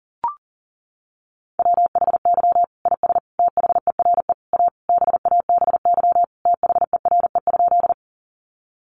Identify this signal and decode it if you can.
Morse code, heard as a keyed tone, about 40 wpm, 700 Hz. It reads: WHY IS THERE A BABY THERE?
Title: Sentences from Top 1300 Words 40wpm